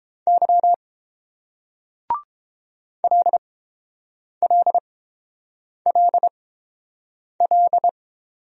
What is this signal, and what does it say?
Morse code, heard as a keyed tone, about 30 wpm, spaced as for 7 wpm, 700 Hz. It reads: Y LLLL